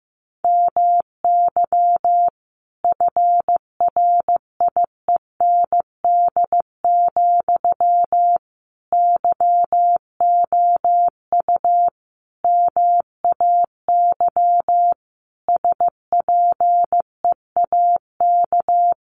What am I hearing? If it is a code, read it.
Morse code, heard as a keyed tone, about 15 wpm, 700 Hz. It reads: MY FRIEND, YOU MAY SPEAK
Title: Sentences from Top 1300 Words 15wpm